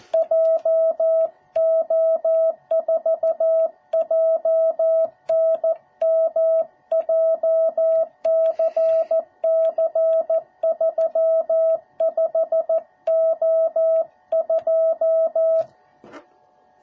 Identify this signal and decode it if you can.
Morse code, heard as a keyed tone, about 14 wpm, 650 Hz. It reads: JO4JNM JCC35O2